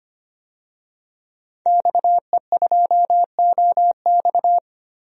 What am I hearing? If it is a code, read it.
Morse code, heard as a keyed tone, about 25 wpm, 700 Hz. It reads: XE2OX
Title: DXCC-100 Call Signs - 5 Characters - 25wpm